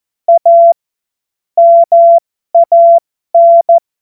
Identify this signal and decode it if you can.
Morse code, heard as a keyed tone, about 14 wpm, spaced as for 12 wpm, 680 Hz. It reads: A MAN
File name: Morsekurs / Witze